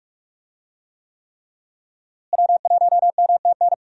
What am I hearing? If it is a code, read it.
Morse code, heard as a keyed tone, about 46 wpm, 700 Hz. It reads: W1MTN
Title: USA Call Signs - 5 Characters - Repeated 3x - 45wpm